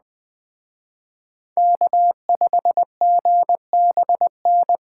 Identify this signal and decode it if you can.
Morse code, heard as a keyed tone, about 20 wpm, 700 Hz. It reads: K5GBN
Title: Speed Racing - USA 1x3 Call Signs 17wpm